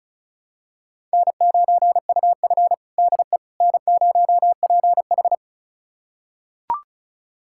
Morse code, heard as a keyed tone, about 36 wpm, 700 Hz. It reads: N9UF DE N0PH
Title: USA Callsign DE Callsign - 4 Characters - 35wpm